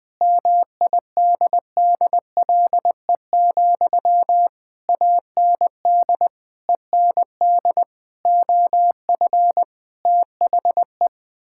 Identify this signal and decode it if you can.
Morse code, heard as a keyed tone, about 20 wpm, 700 Hz. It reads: MIDDLE, AND END OF THE
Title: Sentences from Top 1000 Words - Review Part 6 - 20wpm